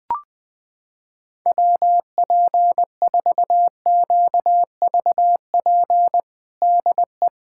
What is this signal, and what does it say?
Morse code, heard as a keyed tone, about 20 wpm, 700 Hz. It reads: WP4QVP DE
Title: USA Callsign DE Callsign 20wpm